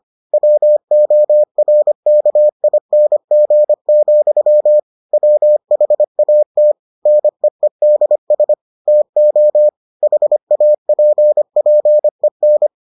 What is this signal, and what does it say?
Morse code, heard as a keyed tone, about 25 wpm, 600 Hz. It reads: WORKING, WHAT NEEDS TO HAPPEN